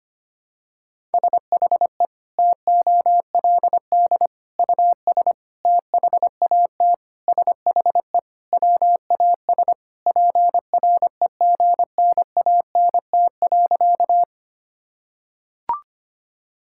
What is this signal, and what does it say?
Morse code, heard as a keyed tone, about 25 wpm, 700 Hz. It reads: SHE TOLD US THAT SHE WAS PREGNANT.
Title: Sentences from Top 1000 Words - Review Part 6 - 25wpm